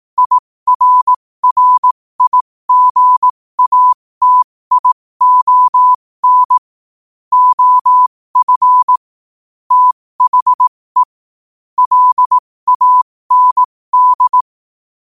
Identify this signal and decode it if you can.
Morse code, heard as a keyed tone, about 18 wpm, spaced as for 15 wpm, 1000 Hz. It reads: IRRIGATION OF THE LAND
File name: qotdfiles / 15wpm